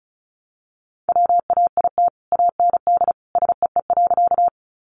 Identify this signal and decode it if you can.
Morse code, heard as a keyed tone, about 34 wpm, 700 Hz. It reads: WAIT AND SEE.
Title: Sentences from Top 500 Words 35wpm